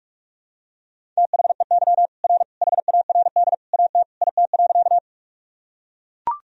Morse code, heard as a keyed tone, about 44 wpm, 700 Hz. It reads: THEY R HARD AT IT.
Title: Sentences from Top 300 Words and Abbreviations 45wpm